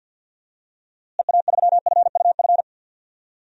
Morse code, heard as a keyed tone, about 50 wpm, 700 Hz. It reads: EA3FUF
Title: DXCC-100 Call Signs - 6 Characters - Repeated 3x - 50wpm